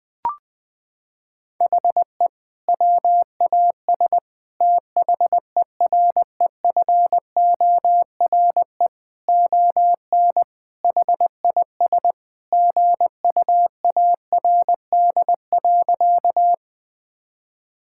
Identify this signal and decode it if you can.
Morse code, heard as a keyed tone, about 20 wpm, 700 Hz. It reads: HE WAS THEREFORE ON HIS GUARD.
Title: Sentences from Top 1000 Words - Review Part 5 - 20wpm